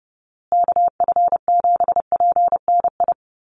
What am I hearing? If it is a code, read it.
Morse code, heard as a keyed tone, about 30 wpm, 700 Hz. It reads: KF7PNI
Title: USA Call Signs Repeated 3x 30wpm